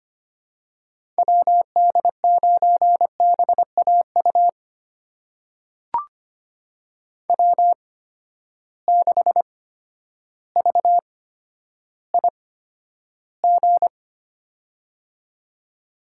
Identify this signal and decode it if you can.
Morse code, heard as a keyed tone, about 25 wpm, spaced as for 11 wpm, 700 Hz. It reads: WD9BAU W6VIG
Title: USA Call Signs - 8x Character Spacing - 25wpm